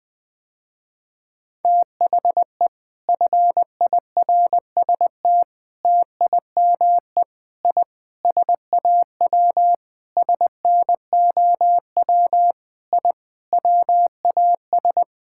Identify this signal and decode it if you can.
Morse code, heard as a keyed tone, about 20 wpm, 700 Hz. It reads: THE FIRST TIME I SAW SNOW I WAS
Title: Sentences with 4-Word Phrases - Spoken Beginning - 20wpm